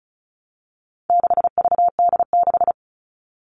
Morse code, heard as a keyed tone, about 34 wpm, 700 Hz. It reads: 6VD6
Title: All Cut Numbers - 4-Digit 35wpm